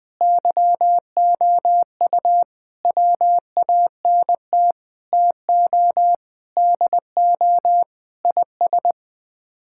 Morse code, heard as a keyed tone, about 21 wpm, 700 Hz. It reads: YOU WANT TO DO IS
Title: Sets of 8 Words - 20wpm